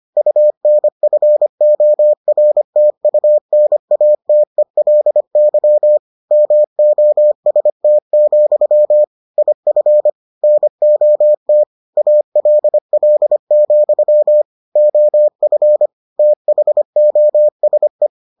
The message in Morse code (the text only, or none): UNFORTUNATELY MOST, IF NOT ALL, OF THOSE